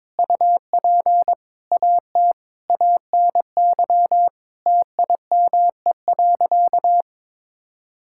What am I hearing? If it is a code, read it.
Morse code, heard as a keyed tone, about 22 wpm, 700 Hz. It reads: UP AT ANY TIME.
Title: Sentences from Top 1000 Words - Review Part 6 - 22wpm